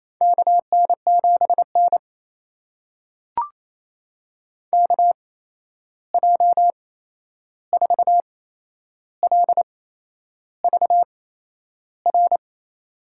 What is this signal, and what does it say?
Morse code, heard as a keyed tone, about 27 wpm, spaced as for 10 wpm, 700 Hz. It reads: KN7N KJ4LVR
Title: USA Call Signs - 8x Character Spacing - 28wpm